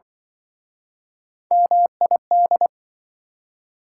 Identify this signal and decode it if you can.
Morse code, heard as a keyed tone, about 28 wpm, 700 Hz. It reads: MID